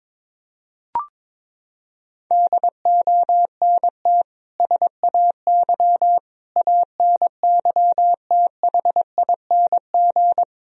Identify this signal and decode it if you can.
Morse code, heard as a keyed tone, about 22 wpm, 700 Hz. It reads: DONT SAY ANYTHING
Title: Sentences from Top 1000 Words - Review Part 1 - 22wpm